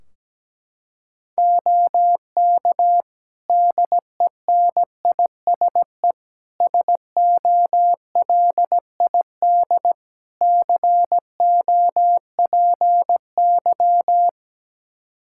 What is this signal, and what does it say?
Morse code, heard as a keyed tone, about 17 wpm, 700 Hz. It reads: OK DENISE SOLID COPY
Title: Practice QSO - Exchange 3 - More Info - 17wpm